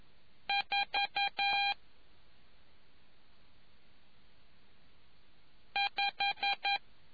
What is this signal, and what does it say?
Morse code, heard as a keyed tone, about 11 wpm, spaced as for 2 wpm, 800 Hz. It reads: 45